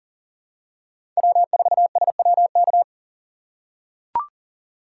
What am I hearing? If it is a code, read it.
Morse code, heard as a keyed tone, about 40 wpm, 700 Hz. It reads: W4SWK
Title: USA Call Signs - 5 Characters - 3x Character Spacing - 40wpm